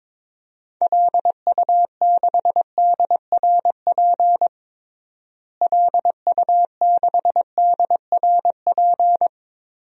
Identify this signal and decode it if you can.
Morse code, heard as a keyed tone, about 22 wpm, 700 Hz. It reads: LU6DRP LU6DRP
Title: DXCC-100 Call Signs - 6 Characters - Repeated 3x - 22wpm